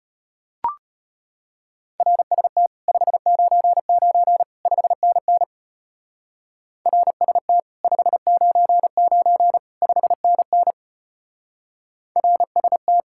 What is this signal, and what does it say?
Morse code, heard as a keyed tone, about 30 wpm, 700 Hz. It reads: RST 599 5NN RST 599 5NN RST
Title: Speed-Racing Common QSO Phrases - 25wpm